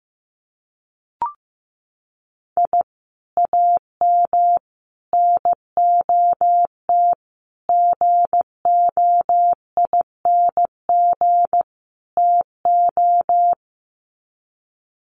Morse code, heard as a keyed tone, about 15 wpm, 700 Hz. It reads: I AM NOT GOING TO
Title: Sets of 5 Words Encore 15wpm